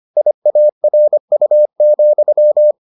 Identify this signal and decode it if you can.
Morse code, heard as a keyed tone, about 25 wpm, 600 Hz. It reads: IARU,